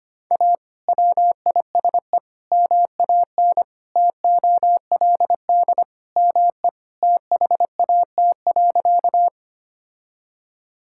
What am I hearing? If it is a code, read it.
Morse code, heard as a keyed tone, about 25 wpm, 700 Hz. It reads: A WISE MAN TOLD ME THAT.
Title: Sentences from Top 1600 Words 25wpm